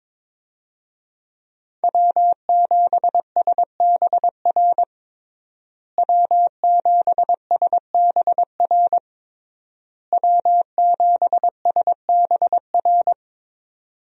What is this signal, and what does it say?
Morse code, heard as a keyed tone, about 22 wpm, 700 Hz. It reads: W7SBR W7SBR W7SBR
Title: USA Call Signs - 5 Characters - Repeated 3x - 22wpm